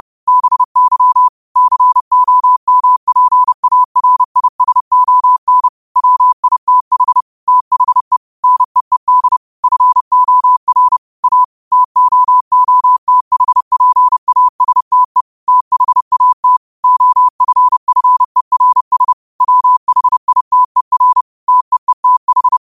Morse code, heard as a keyed tone, about 30 wpm, 1000 Hz. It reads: BY COMPARISON WITH THE NEED FOR A TOOTHPASTE THAT OFFERS WHITER TEETH
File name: qotdfiles / 30wpm